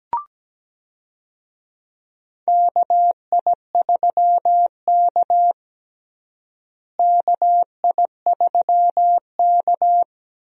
Morse code, heard as a keyed tone, about 17 wpm, 700 Hz. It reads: KI3K KI3K
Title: USA Call Signs - 4 Characters - Repeated 3x - 17wpm